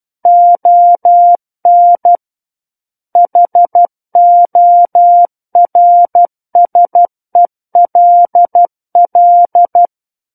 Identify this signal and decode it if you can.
Morse code, heard as a keyed tone, about 12 wpm, 700 Hz. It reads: ON HORSELL
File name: Book One / 03-Chapter 3 - 12 WPM